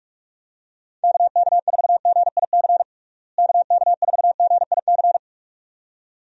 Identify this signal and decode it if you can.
Morse code, heard as a keyed tone, about 46 wpm, 700 Hz. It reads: KK4GIC KK4GIC
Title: USA Call Signs - 6 Characters - Repeated 2x - 45wpm